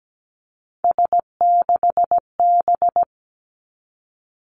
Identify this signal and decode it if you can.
Morse code, heard as a keyed tone, about 17 wpm, 700 Hz. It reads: S6B
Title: Three Challenging Letter-Number Combinations 17wpm